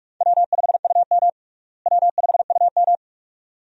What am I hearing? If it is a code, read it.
Morse code, heard as a keyed tone, about 41 wpm, 700 Hz. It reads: W5UM W5UM
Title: USA Call Signs - 4 Characters - Repeated 2x - 45wpm